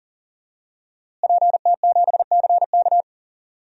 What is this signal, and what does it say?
Morse code, heard as a keyed tone, about 40 wpm, 700 Hz. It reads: PT7CK
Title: DXCC-100 Call Signs - 5 Characters - Repeated 2x - 40wpm